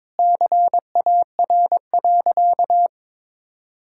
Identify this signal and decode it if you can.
Morse code, heard as a keyed tone, about 22 wpm, 700 Hz. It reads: CAR.
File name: practice / 22wpm / Sentences with 4-Word Phrases - 22wpm